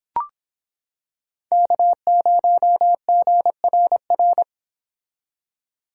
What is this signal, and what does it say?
Morse code, heard as a keyed tone, about 26 wpm, 700 Hz. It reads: K0GRR